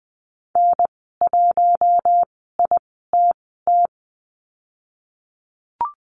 Morse code, heard as a keyed tone, about 20 wpm, spaced as for 15 wpm, 700 Hz. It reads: N1ITT
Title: 042 Lesson 8g - Number 1 - Call Signs 20wpm